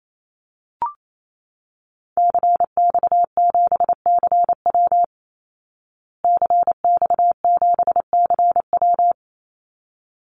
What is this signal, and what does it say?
Morse code, heard as a keyed tone, about 28 wpm, 700 Hz. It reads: CX7CW CX7CW
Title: Call Signs Repeated 3x - 28wpm